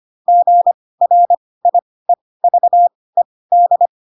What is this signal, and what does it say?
Morse code, heard as a keyed tone, about 25 wpm, spaced as for 18 wpm, 700 Hz. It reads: GRIEVED
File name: Pinocchio ch3 12 25 ik1hge